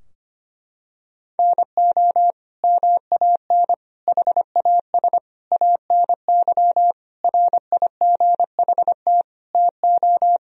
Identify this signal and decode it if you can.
Morse code, heard as a keyed tone, about 25 wpm, 700 Hz. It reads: NO MAN HAS ANY RIGHT TO
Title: Sentences from Top 1000 Words - Review Part 2 - 25wpm